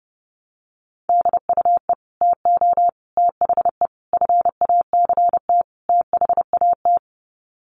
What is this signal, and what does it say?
Morse code, heard as a keyed tone, about 30 wpm, 700 Hz. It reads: DUE TO THE FACT THAT
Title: Sets of 5 Words Encore 30wpm